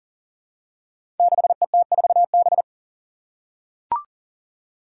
Morse code, heard as a keyed tone, about 37 wpm, 700 Hz. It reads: 6ET4B
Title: Five Challenging Letter-Number Combinations 40wpm